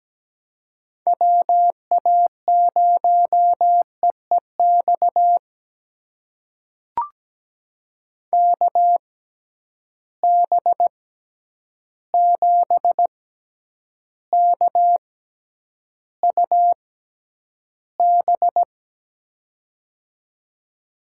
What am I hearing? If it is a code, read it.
Morse code, heard as a keyed tone, about 17 wpm, spaced as for 9 wpm, 700 Hz. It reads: WA0EEX KB7KUB
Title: USA Call Signs - 6 Characters - 6x Character Spacing - 17wpm